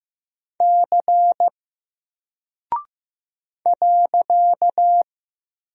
Morse code, heard as a keyed tone, about 15 wpm, 700 Hz. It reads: C .
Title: Single Character (Letter:Number:Punctuation:Prosign) 15wpm